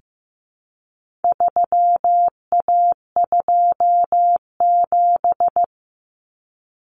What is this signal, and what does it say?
Morse code, heard as a keyed tone, about 15 wpm, 700 Hz. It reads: 3A27
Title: Common Cut Numbers - 4-Digit 15wpm